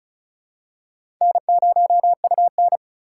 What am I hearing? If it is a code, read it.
Morse code, heard as a keyed tone, about 36 wpm, 700 Hz. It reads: N0UN
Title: USA Call Signs - 4x Character Spacing - 35wpm